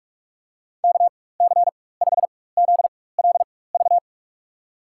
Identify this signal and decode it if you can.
Morse code, heard as a keyed tone, about 42 wpm, spaced as for 20 wpm, 700 Hz. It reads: KC5ZLV
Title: USA Call Signs - 6 Characters - 4x Character Spacing - 45wpm